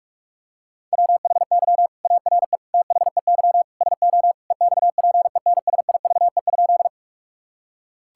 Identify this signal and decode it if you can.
Morse code, heard as a keyed tone, about 41 wpm, 700 Hz. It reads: WHY ARE THEY SO EXPENSIVE?
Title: Sentences from Top 1000 Words - Review Part 6 - 45wpm